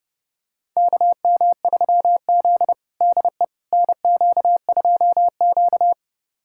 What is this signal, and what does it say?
Morse code, heard as a keyed tone, about 30 wpm, 700 Hz. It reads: KM3Z DE NQ2Q
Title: USA Callsign DE Callsign - 4 Characters - 30wpm